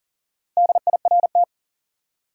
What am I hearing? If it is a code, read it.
Morse code, heard as a keyed tone, about 40 wpm, 700 Hz. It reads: DIRT
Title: Four Letter Words - 40wpm